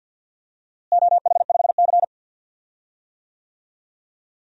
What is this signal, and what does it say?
Morse code, heard as a keyed tone, about 45 wpm, 700 Hz. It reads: OH5C
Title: DXCC-100 Call Signs - 4 Characters - Repeated 3x - 50wpm